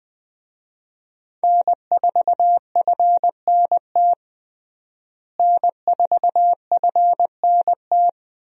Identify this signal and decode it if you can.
Morse code, heard as a keyed tone, about 20 wpm, 700 Hz. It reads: N4FNT N4FNT
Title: USA Call Signs - 5 Characters - Repeated 2x - 20wpm